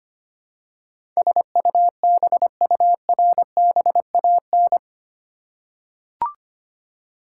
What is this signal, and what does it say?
Morse code, heard as a keyed tone, about 25 wpm, 700 Hz. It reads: SUBURBAN
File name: Eight Letter Words - 25wpm